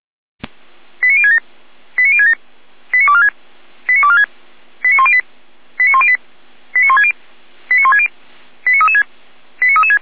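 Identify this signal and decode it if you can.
Five-tone selective call (ZVEI), heard as quick runs of five tones, tones of about 70 ms. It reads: 89067 89067 89246 89246 89108 89108 89160 89160 89336 89336
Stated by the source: ZVEI five-tone selective call 89078 received on an SDR, from feuerwehr-orlamuende.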